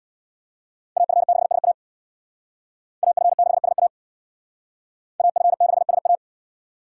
Morse code, heard as a keyed tone, about 63 wpm, 700 Hz. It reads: AF6SA AF6SA AF6SA